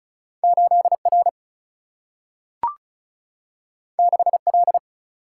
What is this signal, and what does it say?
Morse code, heard as a keyed tone, about 36 wpm, 700 Hz. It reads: 8 R 6 L
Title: Single-Letter Single-Number 35wpm